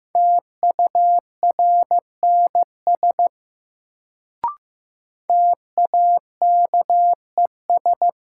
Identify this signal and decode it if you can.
Morse code, heard as a keyed tone, about 15 wpm, 700 Hz. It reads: TURNS TAKES
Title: Sets of 5 Words Encore - New Word Drill - 15wpm